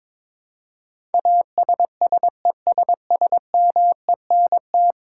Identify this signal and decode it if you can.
Morse code, heard as a keyed tone, about 22 wpm, 700 Hz. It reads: ASSESSMENT
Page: morse-code-ninja-practice-22wpm